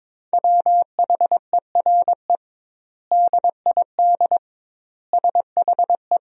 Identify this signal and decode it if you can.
Morse code, heard as a keyed tone, about 22 wpm, 700 Hz. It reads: WHERE DID SHE
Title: 2x Word-Spacing - Sentences from Top 200 Words - 22wpm